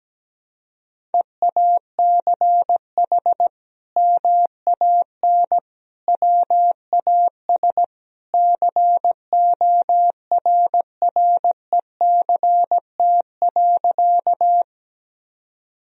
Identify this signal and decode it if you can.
Morse code, heard as a keyed tone, about 17 wpm, 700 Hz. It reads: EACH MAN WAS CORRECT.
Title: Sentences from Top 1000 Words - Review Part 2 - 17wpm